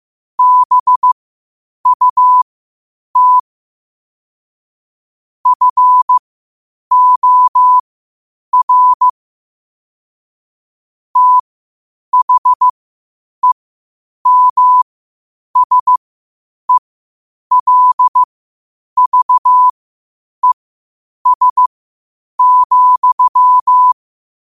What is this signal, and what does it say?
Morse code, heard as a keyed tone, about 15 wpm, spaced as for 8 wpm, 1000 Hz. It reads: BUT FOR THEMSELVES,
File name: qotdfiles / 8wpm